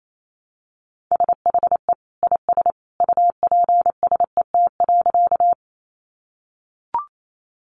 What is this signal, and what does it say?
Morse code, heard as a keyed tone, about 28 wpm, 700 Hz. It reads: SHE IS UPSET.
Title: Sentences from Top 800 Words 28wpm